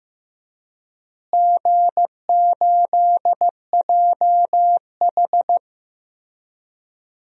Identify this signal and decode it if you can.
Morse code, heard as a keyed tone, about 16 wpm, 700 Hz. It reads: G8JH
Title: DXCC-100 Call Signs - 4 Characters - Repeated 3x - 15wpm